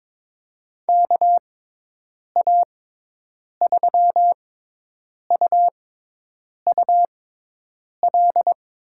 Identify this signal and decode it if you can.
Morse code, heard as a keyed tone, about 23 wpm, spaced as for 8 wpm, 700 Hz. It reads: KA3UUL
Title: USA Call Signs - 6 Characters - 6x Character Spacing - 22wpm